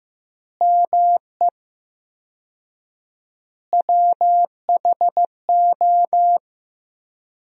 Morse code, heard as a keyed tone, about 15 wpm, 700 Hz. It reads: ME WHO
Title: 4x Word-Spacing - Sentences from Top 200 Words - 15wpm